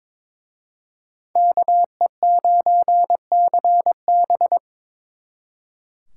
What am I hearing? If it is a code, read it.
Morse code, heard as a keyed tone, about 24 wpm, 700 Hz. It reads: KE9CB
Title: Speed Racing - USA Call Signs 22wpm